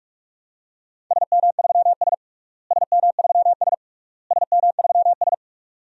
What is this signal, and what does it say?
Morse code, heard as a keyed tone, about 42 wpm, 700 Hz. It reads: SM3S SM3S SM3S
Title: DXCC-100 Call Signs - 4 Characters - Repeated 3x - 45wpm